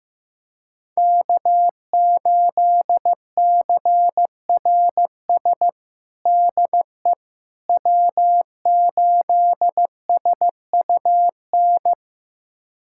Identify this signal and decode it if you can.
Morse code, heard as a keyed tone, about 15 wpm, 700 Hz. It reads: K8CRS DE W8SUN
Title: USA Callsign DE Callsign - 5 Characters - 15wpm